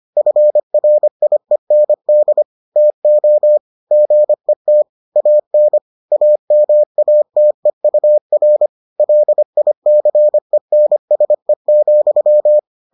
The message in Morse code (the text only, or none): FRIEND TO GET AN AMATEUR LICENSE,